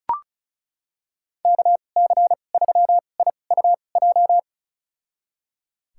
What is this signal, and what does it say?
Morse code, heard as a keyed tone, about 34 wpm, spaced as for 25 wpm, 700 Hz. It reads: KC3IUJ
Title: USA Call Signs - 6 Characters - 2x Character Spacing - 35wpm